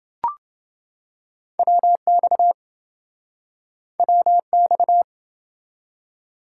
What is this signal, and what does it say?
Morse code, heard as a keyed tone, about 26 wpm, 700 Hz. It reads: WX WX